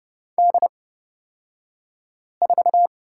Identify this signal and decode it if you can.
Morse code, heard as a keyed tone, about 29 wpm, spaced as for 5 wpm, 700 Hz. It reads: D4
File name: Lesson 36d - BT - Two Characters 30wpm